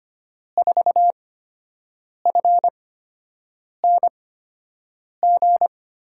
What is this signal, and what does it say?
Morse code, heard as a keyed tone, about 25 wpm, spaced as for 7 wpm, 700 Hz. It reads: 4FNG
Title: USA Call Signs - 6 Characters - 8x Character Spacing - 25wpm